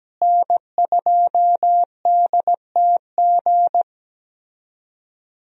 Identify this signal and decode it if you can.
Morse code, heard as a keyed tone, about 17 wpm, 700 Hz. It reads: N2DTG
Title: USA Call Signs - 5 Characters - Repeated 3x - 17wpm